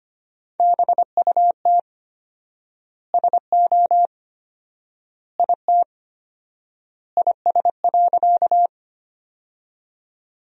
Morse code, heard as a keyed tone, about 25 wpm, 700 Hz. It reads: BUT SO IT IS.
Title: 4x Word-Spacing - Sentences from Top 100 Words - 25wpm